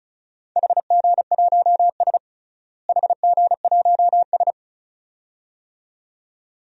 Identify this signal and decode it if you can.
Morse code, heard as a keyed tone, about 36 wpm, 700 Hz. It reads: HG1S HG1S
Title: DXCC-100 Call Signs - 4 Characters - Repeated 2x - 35wpm